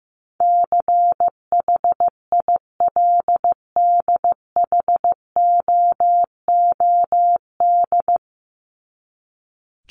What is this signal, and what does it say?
Morse code, heard as a keyed tone, about 15 wpm, 700 Hz. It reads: CHILDHOOD
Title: Top 1751-2000 Words 15wpm